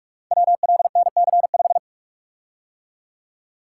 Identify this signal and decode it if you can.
Morse code, heard as a keyed tone, about 44 wpm, 700 Hz. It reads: WLNC5
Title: Code Groups - 5 Letter:Numbers - 45wpm